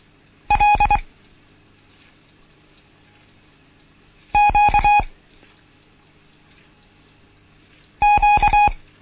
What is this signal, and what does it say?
Morse code, heard as a keyed tone, about 24 wpm, spaced as for 3 wpm, 830 Hz. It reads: LQQ
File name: Morsen / Kurs A 16